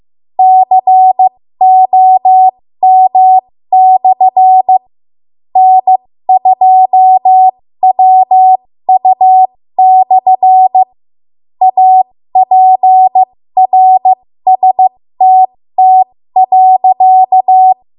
Morse code, heard as a keyed tone, about 15 wpm, spaced as for 12 wpm, 750 Hz. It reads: COM/N2WU/APRSTT.
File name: Morse / 13WPM